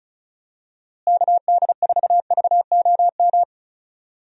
Morse code, heard as a keyed tone, about 34 wpm, 700 Hz. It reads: KD4VOM
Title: USA Call Signs - 6 Characters - Repeated 3x - 35wpm